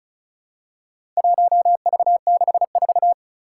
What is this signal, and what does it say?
Morse code, heard as a keyed tone, about 36 wpm, 700 Hz. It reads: 1V64
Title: Four Challenging Letter-Number Combinations 35wpm